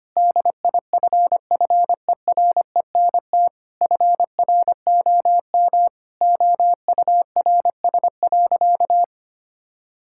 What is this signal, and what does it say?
Morse code, heard as a keyed tone, about 25 wpm, 700 Hz. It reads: DIFFERENT FROM OURS.
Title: Sentences from Top 1000 Words - Review Part 5 - 25wpm